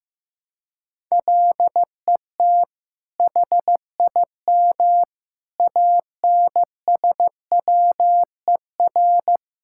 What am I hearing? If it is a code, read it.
Morse code, heard as a keyed tone, about 15 wpm, 700 Hz. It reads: LET HIM ANSWER
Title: Sentences from Top 1000 Words - Review Part 5 - 15wpm